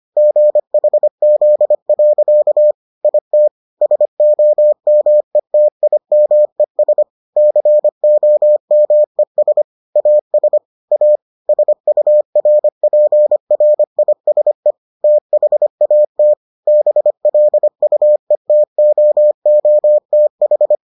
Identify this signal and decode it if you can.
Morse code, heard as a keyed tone, about 25 wpm, 600 Hz. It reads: GHZ. IT SOMETIMES COMES AS A SURPRISE THAT BLUETOOTH